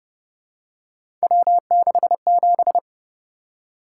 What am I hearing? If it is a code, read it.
Morse code, heard as a keyed tone, about 27 wpm, 700 Hz. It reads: W67